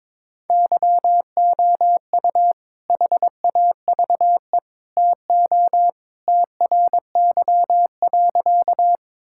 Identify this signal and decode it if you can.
Morse code, heard as a keyed tone, about 22 wpm, 700 Hz. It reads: YOU HAVE TO TRY.
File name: practice / 22wpm / Sentences with 3-Word Phrases - 22wpm